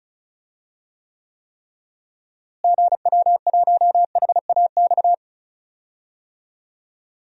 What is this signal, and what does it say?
Morse code, heard as a keyed tone, about 36 wpm, 700 Hz. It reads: GW1HAX
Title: DXCC-100 Call Signs - 6 Characters - Repeated 3x - 35wpm